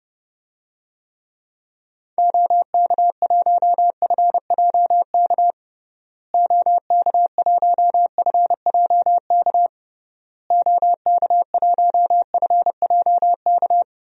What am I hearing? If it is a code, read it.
Morse code, heard as a keyed tone, about 30 wpm, 700 Hz. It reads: OK1FJK OK1FJK OK1FJK
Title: DXCC-100 Call Signs - 6 Characters - Repeated 3x - 30wpm